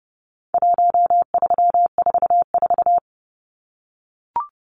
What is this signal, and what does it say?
Morse code, heard as a keyed tone, about 30 wpm, 700 Hz. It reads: 1344
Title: Numbers - 4-Digit 30wpm